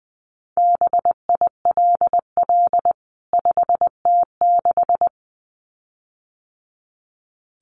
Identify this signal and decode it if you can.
Morse code, heard as a keyed tone, about 20 wpm, 700 Hz. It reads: BILL 5T6
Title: CWT Exchange 20wpm